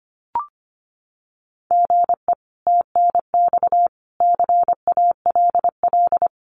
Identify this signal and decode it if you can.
Morse code, heard as a keyed tone, about 25 wpm, 700 Hz. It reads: GE TNX CALL
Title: Common QSO Phrases 25wpm